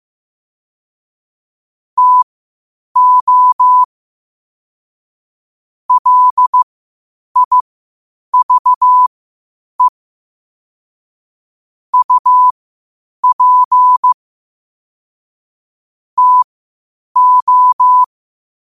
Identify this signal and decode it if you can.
Morse code, heard as a keyed tone, about 15 wpm, spaced as for 8 wpm, 1000 Hz. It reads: TO LIVE UP TO
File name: qotdfiles / 8wpm